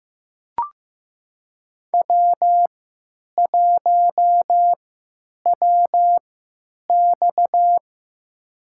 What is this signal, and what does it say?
Morse code, heard as a keyed tone, about 15 wpm, spaced as for 9 wpm, 700 Hz. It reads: W1WX
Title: USA Call Signs - 4 Characters - 3x Character Spacing - 15wpm